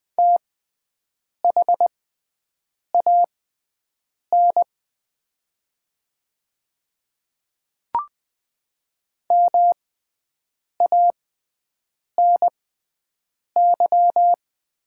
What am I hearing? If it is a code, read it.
Morse code, heard as a keyed tone, about 20 wpm, spaced as for 7 wpm, 700 Hz. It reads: THAN MANY
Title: 6x Character Spacing - Top 200 Words - 20wpm